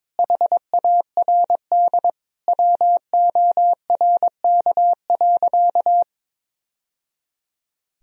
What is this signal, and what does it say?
Morse code, heard as a keyed tone, about 22 wpm, 700 Hz. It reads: HARD WORK.
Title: Sentences from Top 1000 Words - Review Part 2 - 22wpm